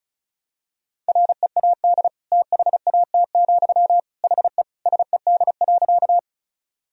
Morse code, heard as a keyed tone, about 36 wpm, 700 Hz. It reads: READ THAT, HE SED.
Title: Sentences from Top 200 Words and Abbreviations 35wpm